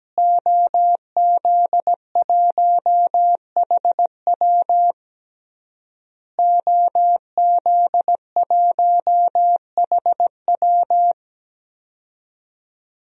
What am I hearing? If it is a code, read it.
Morse code, heard as a keyed tone, about 17 wpm, 700 Hz. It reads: OZ1HW OZ1HW
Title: DXCC-100 Call Signs - 5 Characters - Repeated 3x - 17wpm